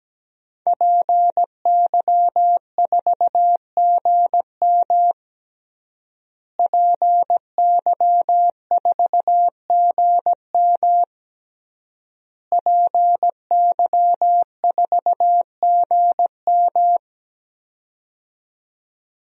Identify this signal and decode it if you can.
Morse code, heard as a keyed tone, about 17 wpm, 700 Hz. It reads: PY4GM PY4GM PY4GM
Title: DXCC-100 Call Signs - 5 Characters - Repeated 3x - 17wpm